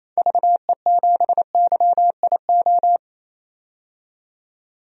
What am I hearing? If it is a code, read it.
Morse code, heard as a keyed tone, about 28 wpm, 700 Hz. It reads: VE7YIO
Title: DXCC-100 Call Signs - 6 Characters - 28wpm